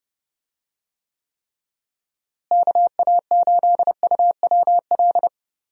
Morse code, heard as a keyed tone, about 32 wpm, 700 Hz. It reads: KA8UWL
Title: USA Call Signs - 6 Characters - Repeated 3x - 30wpm